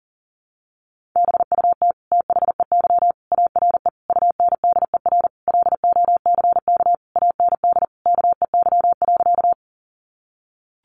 Morse code, heard as a keyed tone, about 40 wpm, 700 Hz. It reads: BUT THEY ARE UNDER LOCK AND KEY.
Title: Sentences from Top 1300 Words 40wpm